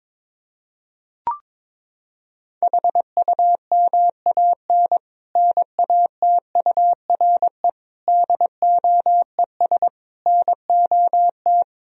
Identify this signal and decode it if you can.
Morse code, heard as a keyed tone, about 22 wpm, 700 Hz. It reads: HUMAN NATURE DOES NOT
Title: Sentences from Top 1000 Words - Review Part 3 - 22wpm